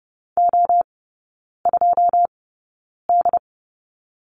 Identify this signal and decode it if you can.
Morse code, heard as a keyed tone, about 30 wpm, spaced as for 9 wpm, 700 Hz. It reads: O2D
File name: 081 Lesson 14e - Number 2 - Three Characters 30wpm